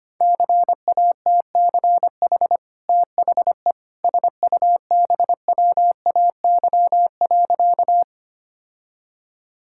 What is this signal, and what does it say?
Morse code, heard as a keyed tone, about 25 wpm, 700 Hz. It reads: CATCH THE SUBWAY.
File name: Easy Sentences with Hints - 25wpm